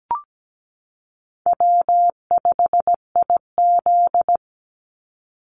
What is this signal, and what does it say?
Morse code, heard as a keyed tone, about 17 wpm, 700 Hz. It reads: W5IZ
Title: Call Signs - 17wpm